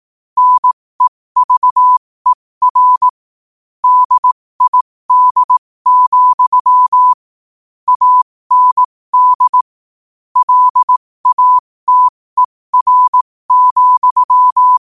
Morse code, heard as a keyed tone, about 18 wpm, spaced as for 15 wpm, 1000 Hz. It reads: NEVER DID, AND LATER,
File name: qotdfiles / 15wpm